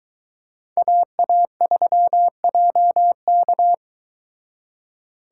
Morse code, heard as a keyed tone, about 23 wpm, 700 Hz. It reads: AA3JK